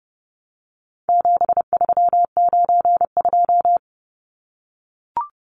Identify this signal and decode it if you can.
Morse code, heard as a keyed tone, about 30 wpm, 700 Hz. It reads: 7392
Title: Numbers - 4-Digit 30wpm